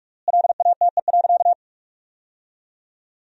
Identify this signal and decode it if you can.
Morse code, heard as a keyed tone, about 41 wpm, 700 Hz. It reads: LATE.
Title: 3x Word-Spacing - Sentences from Top 300 Words - 45wpm